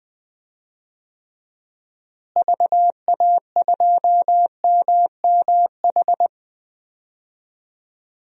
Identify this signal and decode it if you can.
Morse code, heard as a keyed tone, about 20 wpm, 700 Hz. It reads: VA2MMH
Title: DXCC-100 Call Signs - 6 Characters - Repeated 2x - 20wpm